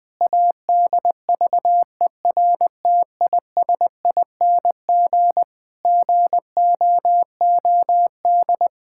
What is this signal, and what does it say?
Morse code, heard as a keyed tone, about 21 wpm, 700 Hz. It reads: ADVERTISING GOOD
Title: Sentences from Top 1000 Words - Review Part 6 - 20wpm